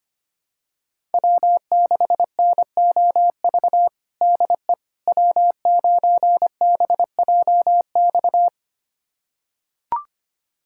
Morse code, heard as a keyed tone, about 25 wpm, 700 Hz. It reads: W6NOV DE W9BJX
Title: USA Callsign DE Callsign - 5 Characters - 25wpm